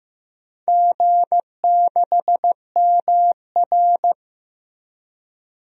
Morse code, heard as a keyed tone, about 15 wpm, 700 Hz. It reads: G6MR
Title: DXCC-100 Call Signs - 4 Characters - 15wpm